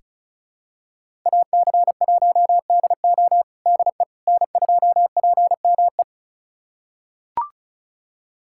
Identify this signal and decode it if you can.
Morse code, heard as a keyed tone, about 36 wpm, 700 Hz. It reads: AC1DO DE N2PME
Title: USA Callsign DE Callsign - 5 Characters - 35wpm